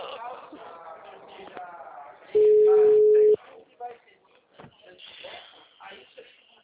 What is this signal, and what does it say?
Ringback tone (Europe, 425 Hz), with speech.